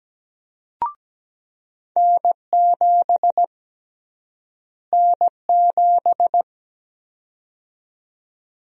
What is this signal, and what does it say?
Morse code, heard as a keyed tone, about 17 wpm, 700 Hz. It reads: N7 N7
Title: USA Call Sign Prefixes - Repeated 2x - 17wpm